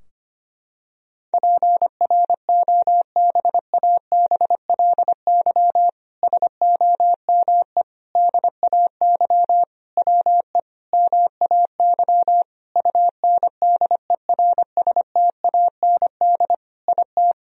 Morse code, heard as a keyed tone, about 25 wpm, 700 Hz. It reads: PROBABLY SOME DAY WE MAY UNDERSTAND IT
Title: Sentences from Top 1000 Words - Review Part 4 - 25wpm